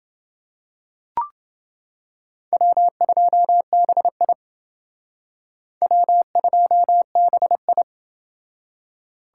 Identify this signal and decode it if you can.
Morse code, heard as a keyed tone, about 28 wpm, 700 Hz. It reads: W2BI W2BI